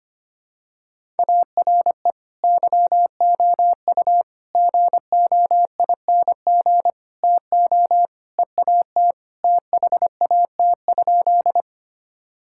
Morse code, heard as a keyed tone, about 25 wpm, 700 Hz. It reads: ARE YOU GOING TO EAT THAT?
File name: practice / 25wpm / Sentences with 4-Word Phrases - 25wpm